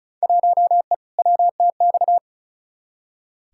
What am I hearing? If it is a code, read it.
Morse code, heard as a keyed tone, about 36 wpm, 700 Hz. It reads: 1E WTX
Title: Field Day Exchanges - 35wpm